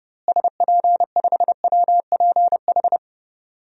Morse code, heard as a keyed tone, about 32 wpm, 700 Hz. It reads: SP5WPH
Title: DXCC-100 Call Signs - 6 Characters - Repeated 3x - 30wpm